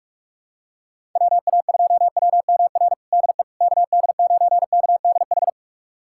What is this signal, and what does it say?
Morse code, heard as a keyed tone, about 45 wpm, 700 Hz. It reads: WA2WMR DE KD9KDH
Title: USA Callsign DE Callsign - 6 Characters - 45wpm